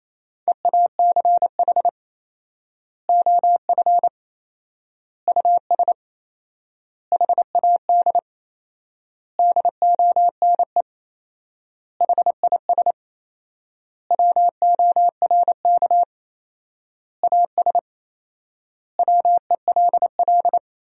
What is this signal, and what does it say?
Morse code, heard as a keyed tone, about 27 wpm, 700 Hz. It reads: EACH OF US HAD DONE HIS WORK AS WELL
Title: 4x Word-Spacing - Sentences from Top 500 Words - 28wpm